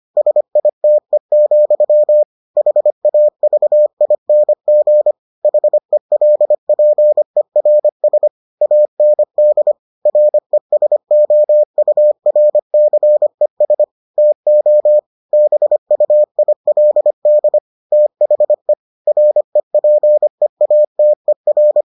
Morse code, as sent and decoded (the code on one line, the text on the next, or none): ... .. - . --..-- / .... .- ...- .. -. --. / .... .-.. .--. .-. ... / .- -. -.. / .-. ... --- ..- .-. -.-. ... / - --- / -... ..- .. .-.. -.. / - .... / .-. .--. .- - . .-.
SITE, HAVING HELPERS AND RESOURCES TO BUILD THE REPEATER